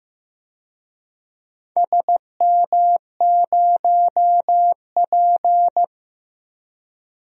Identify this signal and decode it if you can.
Morse code, heard as a keyed tone, about 15 wpm, 700 Hz. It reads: SM0P
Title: DXCC-100 Call Signs - 4 Characters - 15wpm